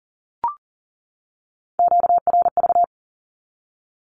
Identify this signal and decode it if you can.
Morse code, heard as a keyed tone, about 40 wpm, 700 Hz. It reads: QRV
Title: Q-Codes 40wpm